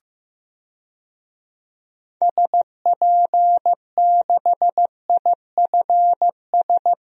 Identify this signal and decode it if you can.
Morse code, heard as a keyed tone, about 15 wpm, 700 Hz. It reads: SP6IFS
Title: DXCC-100 Call Signs - 6 Characters - Repeated 2x - 15wpm